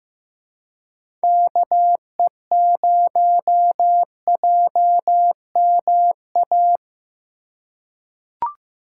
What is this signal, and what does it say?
Morse code, heard as a keyed tone, about 15 wpm, 700 Hz. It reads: KE0JMA
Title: USA Call Signs - 6 Characters - 2x Character Spacing - 15wpm